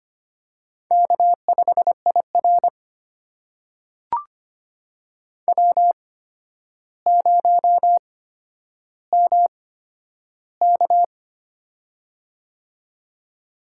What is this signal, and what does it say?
Morse code, heard as a keyed tone, about 25 wpm, spaced as for 11 wpm, 700 Hz. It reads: K5IR W0MK
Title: USA Call Signs - 4 Characters - 8x Character Spacing - 25wpm